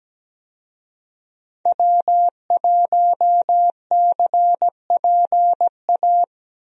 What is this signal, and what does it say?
Morse code, heard as a keyed tone, about 17 wpm, 700 Hz. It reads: W1CPA
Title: USA Call Signs - 5 Characters - Repeated 2x - 17wpm